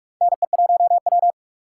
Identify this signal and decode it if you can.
Morse code, heard as a keyed tone, about 46 wpm, 700 Hz. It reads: NE1W
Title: USA Call Signs - 4 Characters - 6x Character Spacing - 45wpm